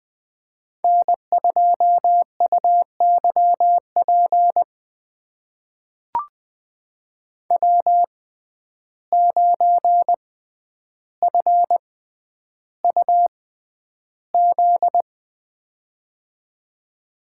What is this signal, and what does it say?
Morse code, heard as a keyed tone, about 20 wpm, spaced as for 10 wpm, 700 Hz. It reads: N2UYP W9FUZ
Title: USA Call Signs - 5 Characters - 6x Character Spacing - 20wpm